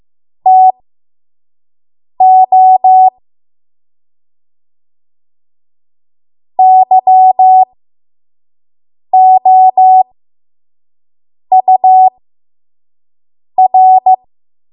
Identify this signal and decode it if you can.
Morse code, heard as a keyed tone, about 15 wpm, spaced as for 5 wpm, 750 Hz. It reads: TO YOUR
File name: Morse / 05WPM